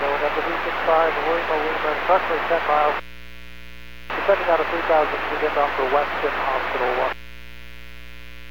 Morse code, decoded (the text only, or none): none